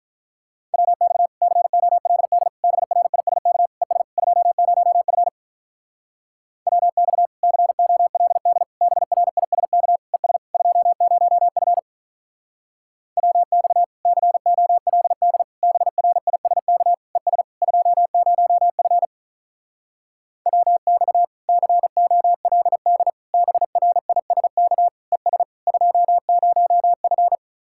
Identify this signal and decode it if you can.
Morse code, heard as a keyed tone, about 46 wpm, 700 Hz. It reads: WX COLD BRISK ES 20F WX COLD BRISK ES 20F WX COLD BRISK ES 20F WX COLD BRISK ES 20F